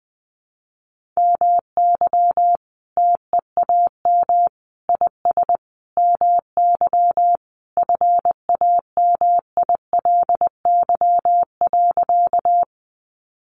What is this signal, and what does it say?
Morse code, heard as a keyed tone, about 20 wpm, 700 Hz. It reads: MY TEAM IS MY FAMILY.
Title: Sentences from Top 1500 Words 20wpm